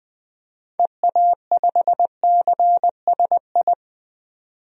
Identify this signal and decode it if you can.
Morse code, heard as a keyed tone, about 21 wpm, 700 Hz. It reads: EA5CSI
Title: DXCC-100 Call Signs - 6 Characters - 20wpm